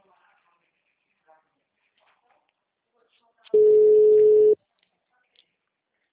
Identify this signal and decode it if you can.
Ringback tone (Europe, 425 Hz), with speech.